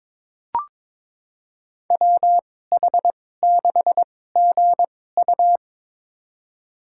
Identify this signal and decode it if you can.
Morse code, heard as a keyed tone, about 22 wpm, spaced as for 16 wpm, 700 Hz. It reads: WH6GU
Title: USA Call Signs - 2x Character Spacing - 22wpm